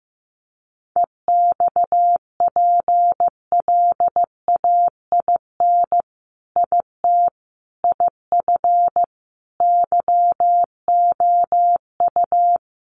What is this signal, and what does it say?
Morse code, heard as a keyed tone, about 15 wpm, 700 Hz. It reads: EXPLAIN IT IF YOU
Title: Sentences from Top 1100 Words 15wpm